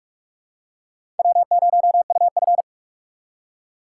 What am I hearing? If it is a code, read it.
Morse code, heard as a keyed tone, about 46 wpm, 700 Hz. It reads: W0UF
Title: K1USN SST QSO Practice - 45wpm